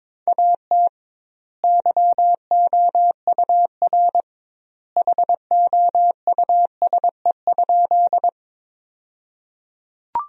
Morse code, heard as a keyed tone, about 22 wpm, 700 Hz. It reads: AT YOUR HOUSE?
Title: 2x Word-Spacing - Sentences from Top 200 Words - 22wpm